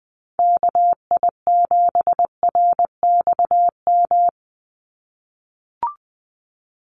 Morse code, heard as a keyed tone, about 20 wpm, 700 Hz. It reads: KI7RXM
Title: USA Call Signs 20wpm